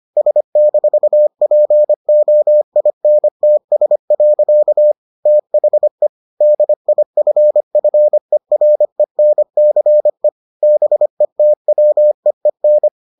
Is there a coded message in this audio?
Yes: Morse code from 0.2 to 12.9 s, heard as a keyed tone, about 25 wpm, 600 Hz. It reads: S-POINTS. THE DIFFERENCE BETWEEN